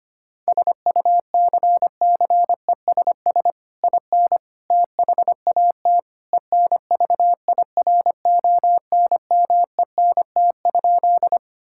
Morse code, heard as a keyed tone, about 25 wpm, 700 Hz. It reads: SUCCESS IN THAT ENVIRONMENT?
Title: Sentences from Top 1000 Words - Review Part 5 - 25wpm